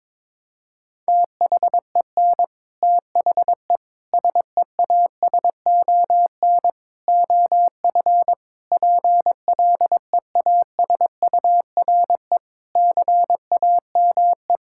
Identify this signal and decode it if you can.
Morse code, heard as a keyed tone, about 22 wpm, 700 Hz. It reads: THEN THE SEASON OF PLEASURE CAME
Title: Sentences from Top 1600 Words 22wpm